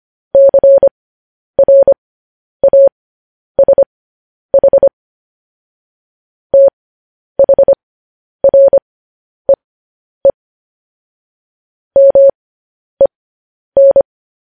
Morse code, heard as a keyed tone, about 25 wpm, spaced as for 10 wpm, 550 Hz. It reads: CRASH THREE MEN
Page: news headlines in morse code at 10 wpm